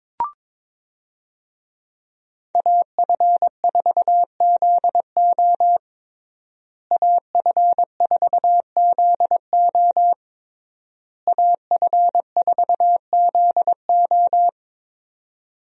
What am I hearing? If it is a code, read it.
Morse code, heard as a keyed tone, about 22 wpm, 700 Hz. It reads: AF4ZO AF4ZO AF4ZO
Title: USA Call Signs - 5 Characters - Repeated 3x - 22wpm